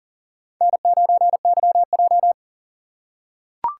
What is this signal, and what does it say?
Morse code, heard as a keyed tone, about 40 wpm, 700 Hz. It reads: N9YJ
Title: USA Call Signs - 4 Characters - 3x Character Spacing - 40wpm